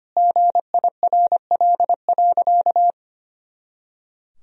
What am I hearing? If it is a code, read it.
Morse code, heard as a keyed tone, about 25 wpm, 700 Hz. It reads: GIRL.
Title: Sentences from Top 400 Words and Abbreviations 25wpm